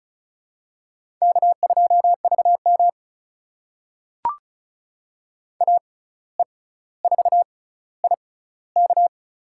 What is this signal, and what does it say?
Morse code, heard as a keyed tone, about 34 wpm, spaced as for 17 wpm, 700 Hz. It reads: K2VM AE4IK
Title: USA Call Signs - 6x Character Spacing - 35wpm